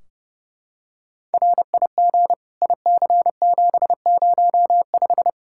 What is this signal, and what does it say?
Morse code, heard as a keyed tone, about 30 wpm, 700 Hz. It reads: RIG IC705
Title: K1USN SST QSO Practice 30wpm